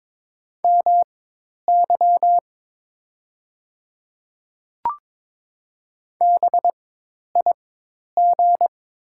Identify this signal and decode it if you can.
Morse code, heard as a keyed tone, about 22 wpm, spaced as for 10 wpm, 700 Hz. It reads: MY BIG
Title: 4x Character Spacing - Top 300 Words - 22wpm